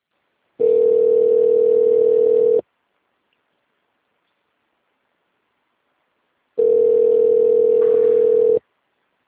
Ringback tone (North America), 440 + 480 Hz.